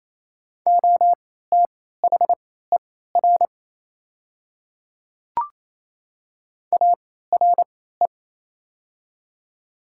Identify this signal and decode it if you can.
Morse code, heard as a keyed tone, about 28 wpm, spaced as for 16 wpm, 690 Hz. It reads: OTHER ARE
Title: 3x Character Spacing - Top 100 Words - 28wpm